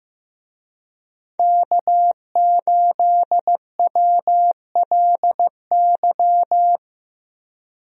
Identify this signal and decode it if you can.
Morse code, heard as a keyed tone, about 15 wpm, 700 Hz. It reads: K8WLY
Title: USA Call Signs - 5 Characters - 15wpm